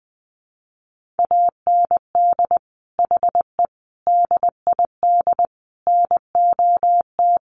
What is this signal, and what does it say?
Morse code, heard as a keyed tone, about 20 wpm, 700 Hz. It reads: AND HE DID NOT
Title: Sentences from Top 1300 Words 20wpm